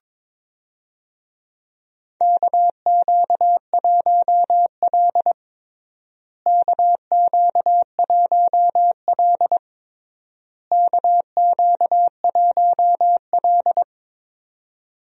Morse code, heard as a keyed tone, about 22 wpm, 700 Hz. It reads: KQ1L KQ1L KQ1L
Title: USA Call Signs - 4 Characters - Repeated 3x - 22wpm